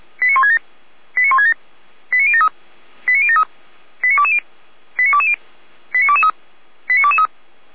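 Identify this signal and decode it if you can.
Five-tone selective call (ZVEI), heard as quick runs of five tones, tones of about 70 ms. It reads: 89167 89167 89073 89073 89220 89220 89223 89223
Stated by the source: ZVEI five-tone selective call received via SDR, on Feuerwehr Orlamünde fire brigade's audio server.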